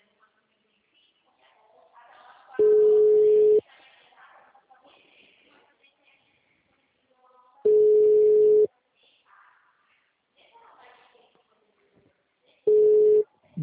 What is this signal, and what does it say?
Ringback tone (Europe, 425 Hz), with speech.
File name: out- -2002- - - .54314